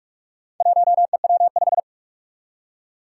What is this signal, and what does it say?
Morse code, heard as a keyed tone, about 44 wpm, 700 Hz. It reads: 1EW5
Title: Four Challenging Letter-Number Combinations 45wpm